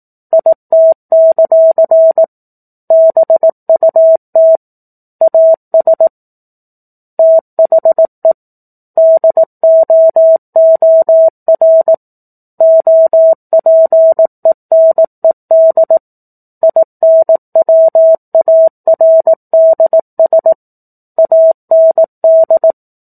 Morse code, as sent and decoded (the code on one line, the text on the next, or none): .. - -.-.-. / -... ..- - / .- ... / - .... / -.. --- --- .-. / --- .--. -. -.. / .. -. .-- .- .-. -.. ... / .- -. -..
IT; BUT AS THE DOOR OPENED INWARDS AND